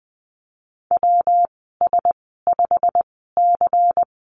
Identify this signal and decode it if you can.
Morse code, heard as a keyed tone, about 20 wpm, spaced as for 15 wpm, 700 Hz. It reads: WS5C
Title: 128 Lesson 21g - Number 6 - Call Signs 20wpm